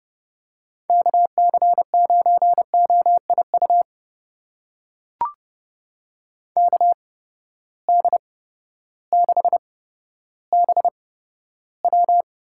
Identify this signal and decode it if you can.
Morse code, heard as a keyed tone, about 32 wpm, spaced as for 14 wpm, 700 Hz. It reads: KC9OIU KD6BW
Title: USA Call Signs - 6 Characters - 8x Character Spacing - 30wpm